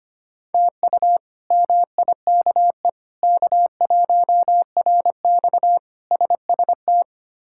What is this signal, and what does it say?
Morse code, heard as a keyed tone, about 25 wpm, 700 Hz. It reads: TU MIKE K1RX SST
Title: K1USN SST QSO Practice - 25wpm